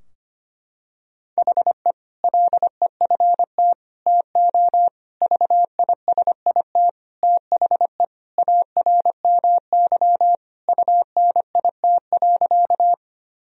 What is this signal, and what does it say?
Morse code, heard as a keyed tone, about 25 wpm, 700 Hz. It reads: HE LEFT TO VISIT THE ARMY UNIT.
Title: Sentences from Top 1000 Words - Review Part 2 - 25wpm